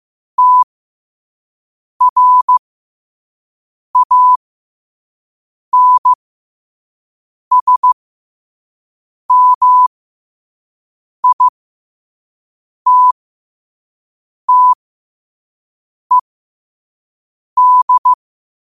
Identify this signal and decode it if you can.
Morse code, heard as a keyed tone, about 15 wpm, spaced as for 5 wpm, 1000 Hz. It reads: TRANSMITTED
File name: qotdfiles / 5wpm